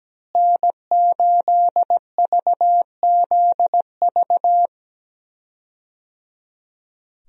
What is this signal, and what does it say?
Morse code, heard as a keyed tone, about 17 wpm, 700 Hz. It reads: N8VZV
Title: USA Call Signs - 5 Characters - Repeated 3x - 17wpm